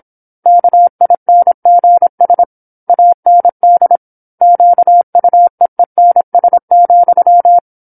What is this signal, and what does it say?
Morse code, heard as a keyed tone, about 27 wpm, 700 Hz. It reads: KINGS AND QUEENS,
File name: 08-Chapter-08 26 WPM